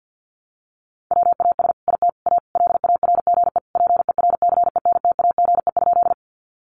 Morse code, heard as a keyed tone, about 49 wpm, 700 Hz. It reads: WAS IT A LARGE PERCENTAGE?
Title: Sentences from Top 1300 Words 50wpm